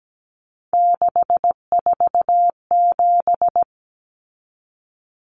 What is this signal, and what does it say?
Morse code, heard as a keyed tone, about 17 wpm, 700 Hz. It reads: 647
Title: Numbers - 3-Digit 17wpm